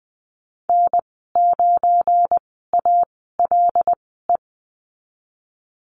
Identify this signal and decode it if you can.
Morse code, heard as a keyed tone, about 20 wpm, spaced as for 14 wpm, 700 Hz. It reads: N9ALE
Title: 179 Lesson 29g - Slash - Call Signs 20wpm